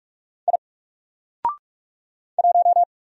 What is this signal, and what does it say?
Morse code, heard as a keyed tone, about 44 wpm, 700 Hz. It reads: I 1
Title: Single Letter-Number Alt 45wpm